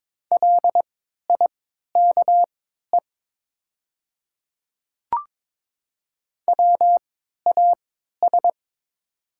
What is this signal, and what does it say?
Morse code, heard as a keyed tone, about 22 wpm, spaced as for 13 wpm, 690 Hz. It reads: LIKE WAS
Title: 3x Character Spacing - Top 100 Words - 22wpm